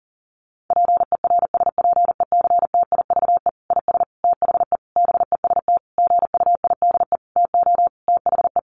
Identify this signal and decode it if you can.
Morse code, heard as a keyed tone, about 40 wpm, 700 Hz. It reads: PERSPECTIVE IS THE BEST GUIDE TO THE
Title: Sentences from Top 1200 Words 40wpm